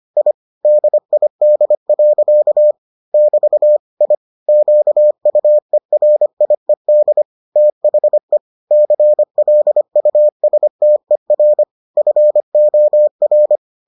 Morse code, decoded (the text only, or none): I DID. <BT> I QUERIED THE CLUSTER FOR